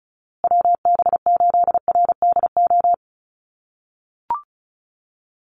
Morse code, heard as a keyed tone, about 34 wpm, 700 Hz. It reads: WB8RDO
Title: USA Call Signs Repeated 2x 35wpm